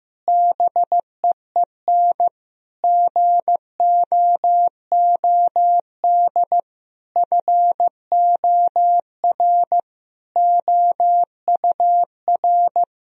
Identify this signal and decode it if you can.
Morse code, heard as a keyed tone, about 15 wpm, 700 Hz. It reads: BEEN GOOD FOR OUR
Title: Sentences from Top 1000 Words - Review Part 4 - 15wpm